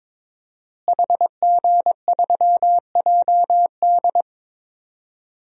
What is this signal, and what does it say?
Morse code, heard as a keyed tone, about 22 wpm, 700 Hz. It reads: HG3JD
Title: DXCC-100 Call Signs - 5 Characters - Repeated 3x - 22wpm